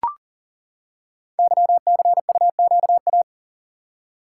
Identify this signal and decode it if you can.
Morse code, heard as a keyed tone, about 38 wpm, 700 Hz. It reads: YCUQA
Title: Code Groups - 5 Letters - 40wpm